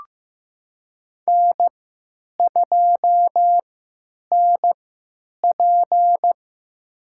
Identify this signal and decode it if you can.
Morse code, heard as a keyed tone, about 15 wpm, spaced as for 9 wpm, 700 Hz. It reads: N2NP
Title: USA Call Signs - 4 Characters - 3x Character Spacing - 15wpm